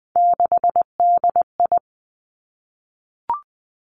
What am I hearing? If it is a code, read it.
Morse code, heard as a keyed tone, about 20 wpm, 700 Hz. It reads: 6DI
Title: Three Challenging Letter-Number Combinations 20wpm